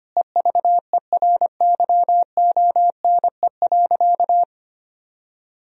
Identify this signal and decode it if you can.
Morse code, heard as a keyed tone, about 25 wpm, 700 Hz. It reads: EVERYONE.
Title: Sentences from Top 1000 Words - Review Part 3 - 25wpm